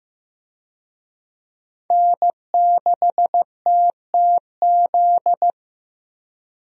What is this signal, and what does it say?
Morse code, heard as a keyed tone, about 15 wpm, 700 Hz. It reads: N6TTZ
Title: USA Call Signs - 5 Characters - Repeated 3x - 15wpm